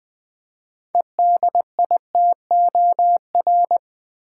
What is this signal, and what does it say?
Morse code, heard as a keyed tone, about 20 wpm, 690 Hz. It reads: EDITOR
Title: 6 Letter Words - First 3 Letters Given - 20wpm